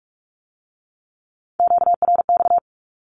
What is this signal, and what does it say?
Morse code, heard as a keyed tone, about 46 wpm, 700 Hz. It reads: QRX
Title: Q-Codes with Spoken Meaning - 45wpm